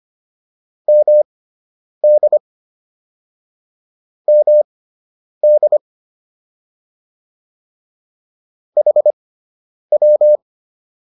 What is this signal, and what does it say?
Morse code, heard as a keyed tone, about 25 wpm, spaced as for 9 wpm, 600 Hz. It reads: MD MD HW